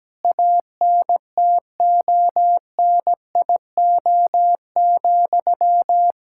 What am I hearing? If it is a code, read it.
Morse code, heard as a keyed tone, about 17 wpm, 700 Hz. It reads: ANTONIO,
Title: Practice QSO - Exchange 1 - Beginning - 17wpm